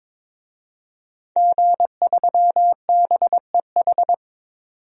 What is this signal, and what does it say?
Morse code, heard as a keyed tone, about 21 wpm, 700 Hz. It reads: G3BEH
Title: DXCC-100 Call Signs - 5 Characters - 22wpm